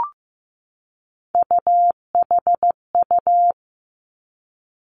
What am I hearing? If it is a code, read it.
Morse code, heard as a keyed tone, about 15 wpm, 700 Hz. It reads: UHU
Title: Three Challenging Letter-Number Combinations 15wpm